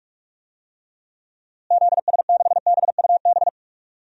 Morse code, heard as a keyed tone, about 46 wpm, 700 Hz. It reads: ZS6BUB
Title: DXCC-100 Call Signs - 6 Characters - Repeated 2x - 45wpm